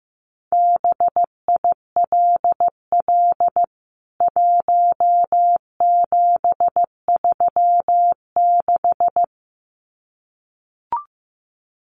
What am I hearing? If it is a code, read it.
Morse code, heard as a keyed tone, about 14 wpm, 700 Hz. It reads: BILL 1736
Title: CWT Exchange 15wpm